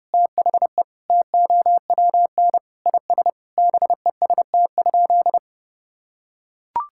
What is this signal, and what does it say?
Morse code, heard as a keyed tone, about 32 wpm, 700 Hz. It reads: THE TOWN IS BEST?
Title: Sentences from Top 1000 Words - Review Part 5 - 30wpm